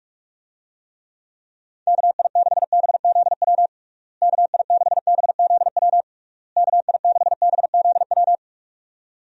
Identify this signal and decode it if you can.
Morse code, heard as a keyed tone, about 41 wpm, 700 Hz. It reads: KI6BZW KI6BZW KI6BZW
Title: USA Call Signs - 6 Characters - Repeated 3x - 45wpm